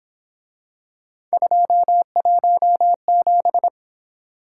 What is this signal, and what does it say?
Morse code, heard as a keyed tone, about 27 wpm, 700 Hz. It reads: 217
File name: Speed-Racing - Three Letter-Number Combinations - 22wpm